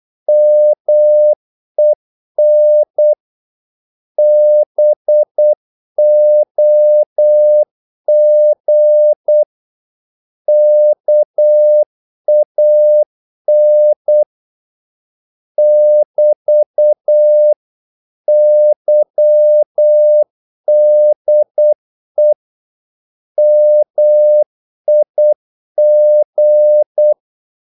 Morse code, heard as a keyed tone, about 8 wpm, 600 Hz. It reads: MEN BOG KAN <BT> YDE MIG
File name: Korte ord DK 8wpm